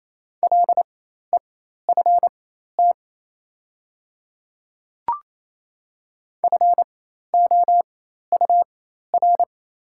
Morse code, heard as a keyed tone, about 28 wpm, spaced as for 13 wpm, 700 Hz. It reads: LEFT FOUR
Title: 4x Character Spacing - Top 300 Words - 28wpm